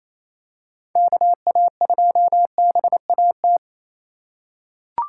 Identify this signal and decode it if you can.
Morse code, heard as a keyed tone, about 28 wpm, 700 Hz. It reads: KA2BAT
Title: USA Call Signs - 6 Characters - 2x Character Spacing - 28wpm